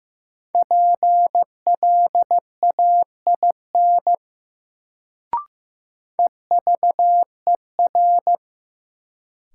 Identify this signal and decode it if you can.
Morse code, heard as a keyed tone, about 15 wpm, 700 Hz. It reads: PLAIN EVER
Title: Sentences from Top 400 Words - New Word Drill - 15wpm